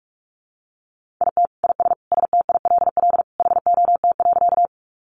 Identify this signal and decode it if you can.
Morse code, heard as a keyed tone, about 44 wpm, 700 Hz. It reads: IT IS STILL HOT.
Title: Additional Easy Phrases 45wpm